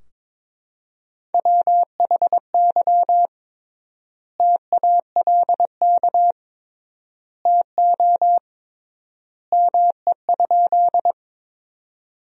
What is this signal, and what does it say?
Morse code, heard as a keyed tone, about 21 wpm, 700 Hz. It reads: WHY TALK TO ME?
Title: 3x Word-Spacing - Sentences from Top 400 Words - 22wpm